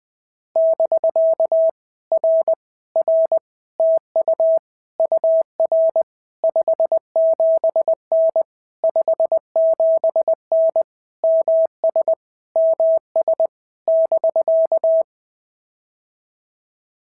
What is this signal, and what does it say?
Morse code, heard as a keyed tone, about 18 wpm, 650 Hz. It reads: <BK> R R TU UR 57N 57N MS MS <BK>
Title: POTA QSO Practice - 20wpm